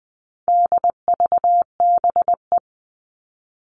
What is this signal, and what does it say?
Morse code, heard as a keyed tone, about 20 wpm, 700 Hz. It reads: DVBE
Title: All Cut Numbers - 4-Digit 20wpm